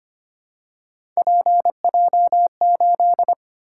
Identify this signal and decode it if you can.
Morse code, heard as a keyed tone, about 25 wpm, 700 Hz. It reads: PJ8
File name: Three Letter-Number Combinations - 25wpm